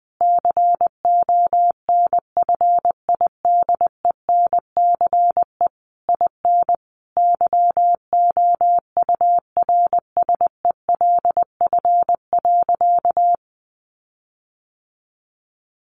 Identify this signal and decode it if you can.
Morse code, heard as a keyed tone, about 20 wpm, 700 Hz. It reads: CONFIDENCE IN YOURSELF.
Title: Sentences from Top 1500 Words 20wpm